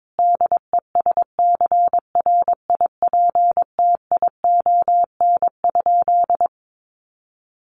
Morse code, heard as a keyed tone, about 22 wpm, 700 Hz. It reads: DESCRIPTION?
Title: Sentences from Top 1500 Words 22wpm